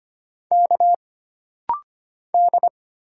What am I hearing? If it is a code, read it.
Morse code, heard as a keyed tone, about 25 wpm, 700 Hz. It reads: K D
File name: Single Character (Letter:Number:Punctuation:Prosign) - Rapid-Fire - 25wpm